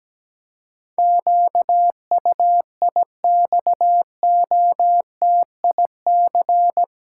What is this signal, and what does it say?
Morse code, heard as a keyed tone, about 17 wpm, 700 Hz. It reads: QUIXOTIC
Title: Pangram Challenge v2 - 17wpm